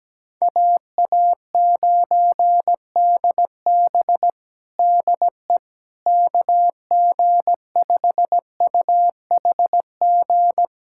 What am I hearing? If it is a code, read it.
Morse code, heard as a keyed tone, about 17 wpm, 700 Hz. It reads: AA9DB DE KG5UHG
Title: USA Callsign DE Callsign 17wpm